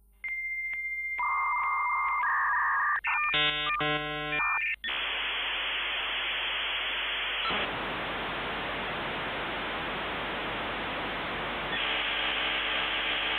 Dial-up modem handshake.
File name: Modem-56k